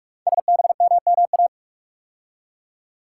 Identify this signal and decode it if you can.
Morse code, heard as a keyed tone, about 44 wpm, 700 Hz. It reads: SBMMA
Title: Code Groups - 5 Letters - 45wpm